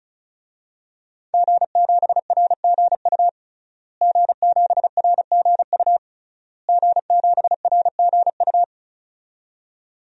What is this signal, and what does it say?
Morse code, heard as a keyed tone, about 34 wpm, 700 Hz. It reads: G7RGU G7RGU G7RGU
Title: DXCC-100 Call Signs - 5 Characters - Repeated 3x - 35wpm